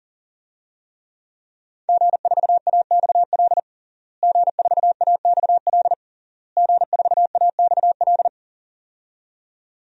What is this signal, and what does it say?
Morse code, heard as a keyed tone, about 40 wpm, 700 Hz. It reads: G4AXL G4AXL G4AXL
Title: DXCC-100 Call Signs - 5 Characters - Repeated 3x - 40wpm